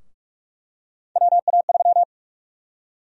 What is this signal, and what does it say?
Morse code, heard as a keyed tone, about 43 wpm, 700 Hz. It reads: WA3
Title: USA Call Sign Prefixes - 45wpm